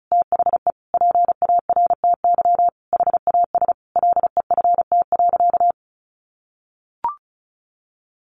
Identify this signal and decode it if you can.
Morse code, heard as a keyed tone, about 34 wpm, 700 Hz. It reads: THE PARTY HAS LEFT.
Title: Sentences from Top 1500 Words 35wpm